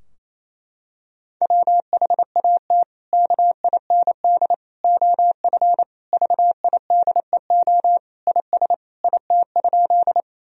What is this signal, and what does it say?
Morse code, heard as a keyed tone, about 28 wpm, 700 Hz. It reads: WHAT KIND OF VIDEO IS IT?
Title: Sentences from Top 1000 Words - Review Part 4 - 28wpm